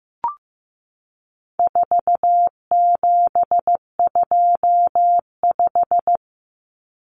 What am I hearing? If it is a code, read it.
Morse code, heard as a keyed tone, about 15 wpm, 700 Hz. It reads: 4725
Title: Numbers - 4-Digit 15wpm